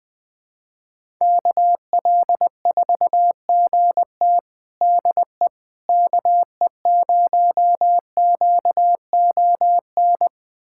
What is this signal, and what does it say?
Morse code, heard as a keyed tone, about 20 wpm, 700 Hz. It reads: KL4GT DE KE0QON
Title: USA Callsign DE Callsign 20wpm